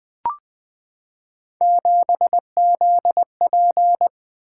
Morse code, heard as a keyed tone, about 20 wpm, 700 Hz. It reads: 7ZP
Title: Three Letter-Number Combinations - 20wpm